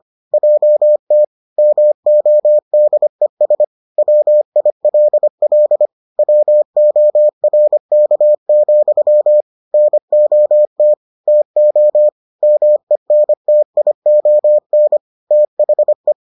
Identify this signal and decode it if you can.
Morse code, heard as a keyed tone, about 25 wpm, 600 Hz. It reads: JT MODES WILL WORK, NOT TO MENTION THE